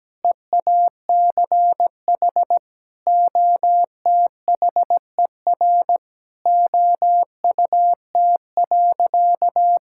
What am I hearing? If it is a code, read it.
Morse code, heard as a keyed tone, about 17 wpm, 700 Hz. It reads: EACH OTHER OUT.
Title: Sentences from Top 1000 Words - Review Part 6 - 17wpm